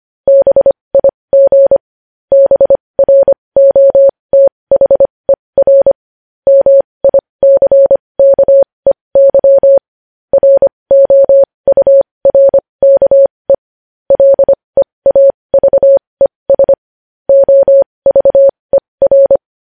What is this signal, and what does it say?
Morse code, heard as a keyed tone, about 25 wpm, spaced as for 20 wpm, 550 Hz. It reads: BIG BROTHER MICKEY ROURKE LEAVES OVER
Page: news-headlines-in-morse-code-at-20-wpm